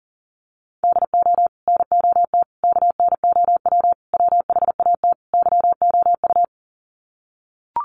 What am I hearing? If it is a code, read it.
Morse code, heard as a keyed tone, about 40 wpm, 700 Hz. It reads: DO NOT KNOW WHAT YOU
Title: Sets of 5 Words 40wpm